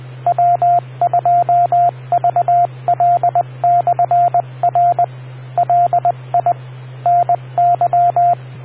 Morse code, heard as a keyed tone, about 20 wpm, 700 Hz, with radio noise. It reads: W2VL/R LI NY